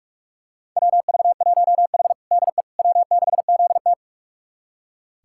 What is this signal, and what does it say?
Morse code, heard as a keyed tone, about 46 wpm, 700 Hz. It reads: WV1H DE W6ZT
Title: USA Callsign DE Callsign - 4 Characters - 45wpm